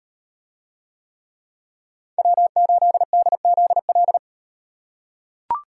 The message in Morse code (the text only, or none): W8DZL